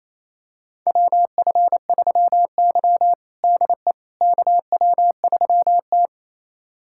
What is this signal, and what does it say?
Morse code, heard as a keyed tone, about 28 wpm, 700 Hz. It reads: WF3Y DE KW3T
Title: USA Callsign DE Callsign - 4 Characters - 28wpm